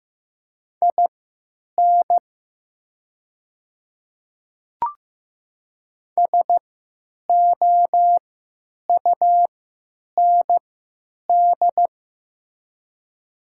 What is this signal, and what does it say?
Morse code, heard as a keyed tone, about 15 wpm, spaced as for 9 wpm, 700 Hz. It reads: IN SOUND
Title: 3x Character Spacing - Top 200 Words - 15wpm